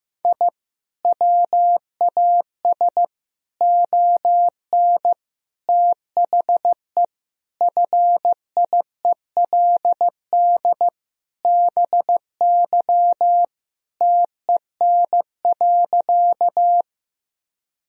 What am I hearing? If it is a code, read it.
Morse code, heard as a keyed tone, about 15 wpm, 700 Hz. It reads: I WAS ON THE FIELD BY TEN.
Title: Sentences from Top 1000 Words - Review Part 2 - 15wpm